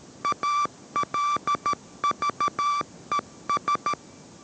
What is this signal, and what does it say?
Morse code, heard as a keyed tone, about 13 wpm, 1160 Hz. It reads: ALVES